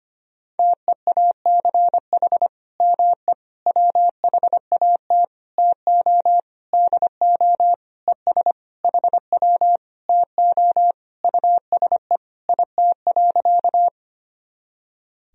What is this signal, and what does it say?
Morse code, heard as a keyed tone, about 25 wpm, 700 Hz. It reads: TEACH ME WHAT TO DO ES HW TO USE IT.
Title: Sentences from Top 500 Words and Abbreviations 25wpm